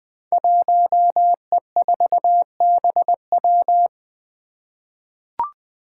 Morse code, heard as a keyed tone, about 20 wpm, 700 Hz. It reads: 1E4BW
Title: Five Challenging Letter-Number Combinations 20wpm